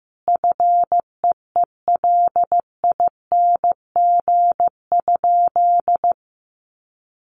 Morse code, heard as a keyed tone, about 15 wpm, 700 Hz. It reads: FEELING?
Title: Sentences from Top 1200 Words 15wpm